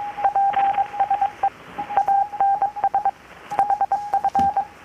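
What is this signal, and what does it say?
Morse code, heard as a keyed tone, about 22 wpm, 790 Hz. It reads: PSE ANS HF
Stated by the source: Morse from KPH's Night of Nights XX maritime radiotelegraphy event, received in Chiapas, Mexico about 3,800 km away on a Kenwood R-600 with a grounded 10 m long-wire antenna.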